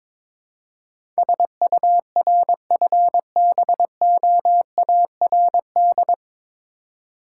Morse code, heard as a keyed tone, about 22 wpm, 690 Hz. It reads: SURFBOARD
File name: Difficult Words with Hints - 22wpm